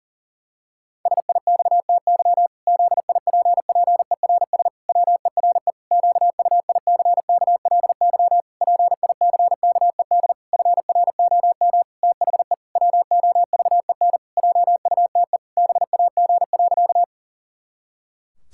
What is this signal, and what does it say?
Morse code, heard as a keyed tone, about 40 wpm, 700 Hz. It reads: SIXTY ZIPPERS WERE QUICKLY PICKED FROM THE WOVEN JUTE BAG.
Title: Pangram Challenge v2 - 40wpm